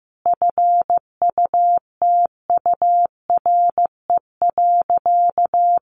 Morse code, heard as a keyed tone, about 15 wpm, 700 Hz. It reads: FUTURE.
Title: Sentences from Top 1400 Words 15wpm